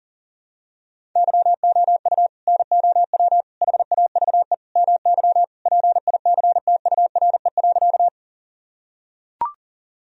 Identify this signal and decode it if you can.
Morse code, heard as a keyed tone, about 40 wpm, 700 Hz. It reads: YOU NOW HAVE MY PICTURE.
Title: Sentences from Top 1000 Words - Review Part 1 - 40wpm